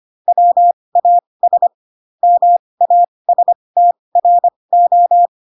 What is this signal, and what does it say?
Morse code, heard as a keyed tone, about 25 wpm, spaced as for 20 wpm, 700 Hz. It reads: WAS MASTRO
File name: Pinocchio / Pinocchio ch1 15 25 ik1hge